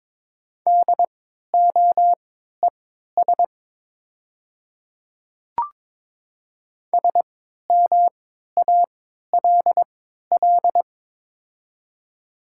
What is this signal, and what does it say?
Morse code, heard as a keyed tone, about 23 wpm, spaced as for 13 wpm, 700 Hz. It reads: DOES SMALL
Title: 3x Character Spacing - Top 300 Words - 22wpm